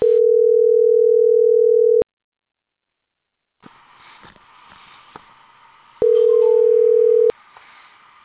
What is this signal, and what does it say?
Ringback tone (North America), 440 + 480 Hz.